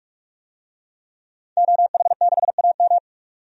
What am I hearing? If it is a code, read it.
Morse code, heard as a keyed tone, about 46 wpm, 700 Hz. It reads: OH6AM
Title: DXCC-100 Call Signs - 5 Characters - 45wpm